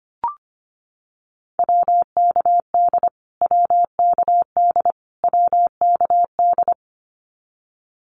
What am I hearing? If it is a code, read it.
Morse code, heard as a keyed tone, about 25 wpm, 700 Hz. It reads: WKD WKD WKD
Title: Abbreviations Repeated 3x - 25wpm